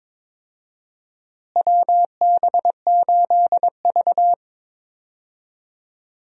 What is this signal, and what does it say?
Morse code, heard as a keyed tone, about 22 wpm, 700 Hz. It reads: WB8V
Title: USA Call Signs - 4 Characters - Repeated 3x - 22wpm